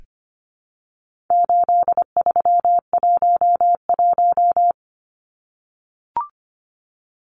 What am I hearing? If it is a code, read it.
Morse code, heard as a keyed tone, about 25 wpm, 700 Hz. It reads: 8311
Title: Numbers - 4-Digit 25wpm